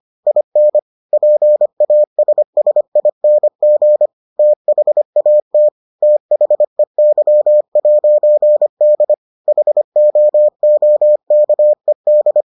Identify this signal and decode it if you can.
Morse code, heard as a keyed tone, about 25 wpm, 600 Hz. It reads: IN PASSING THAT THEY'D HOOKED